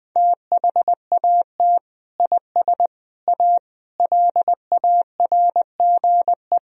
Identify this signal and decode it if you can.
Morse code, heard as a keyed tone, about 20 wpm, 700 Hz. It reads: THAT IS A LARGE
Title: Sentences from Top 1000 Words - Review Part 2 - 20wpm